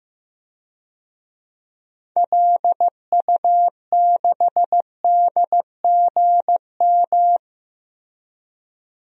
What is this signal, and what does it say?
Morse code, heard as a keyed tone, about 15 wpm, 700 Hz. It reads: LU6DGM
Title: DXCC-100 Call Signs - 6 Characters - 15wpm